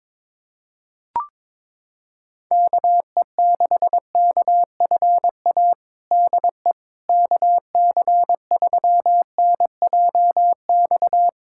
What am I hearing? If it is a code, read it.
Morse code, heard as a keyed tone, about 23 wpm, 700 Hz. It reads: KE6KFA DE KC3NJX
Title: USA Callsign DE Callsign - 6 Characters - 22wpm